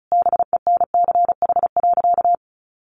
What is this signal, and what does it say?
Morse code, heard as a keyed tone, about 34 wpm, 700 Hz. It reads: BENCH.
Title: Sentences from Top 1400 Words 35wpm